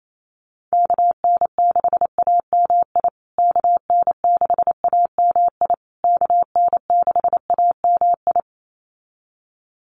Morse code, heard as a keyed tone, about 28 wpm, 690 Hz. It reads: KN6AMI KN6AMI KN6AMI
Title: USA Call Signs Repeated 3x 28wpm